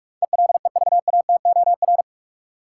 Morse code, heard as a keyed tone, about 45 wpm, 700 Hz. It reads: ELEVATOR
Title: Sentences from Top 1600 Words - New Word Drill Alt - 45wpm